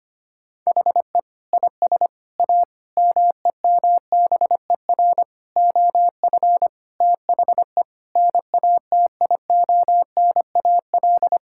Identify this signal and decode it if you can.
Morse code, heard as a keyed tone, about 25 wpm, 700 Hz. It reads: HE IS A MEMBER OF THE NATIONAL
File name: practice / 25wpm / Sentences with 3-Word Phrases - 25wpm